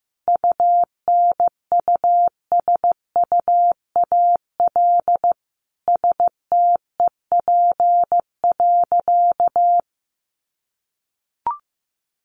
Morse code, heard as a keyed tone, about 15 wpm, 700 Hz. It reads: UNUSUAL STEP.
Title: Sentences from Top 1400 Words 15wpm